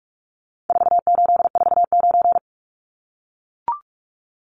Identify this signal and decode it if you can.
Morse code, heard as a keyed tone, about 44 wpm, 700 Hz. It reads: 4849
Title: Numbers - 4-Digit 45wpm